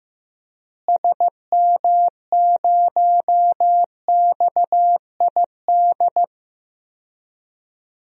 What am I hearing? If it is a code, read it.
Morse code, heard as a keyed tone, about 15 wpm, 700 Hz. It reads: SM0XID
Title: DXCC-100 Call Signs - 6 Characters - Repeated 2x - 15wpm